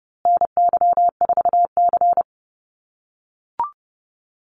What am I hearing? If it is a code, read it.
Morse code, heard as a keyed tone, about 28 wpm, 700 Hz. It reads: NY4C
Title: USA Call Signs Repeated 3x 30wpm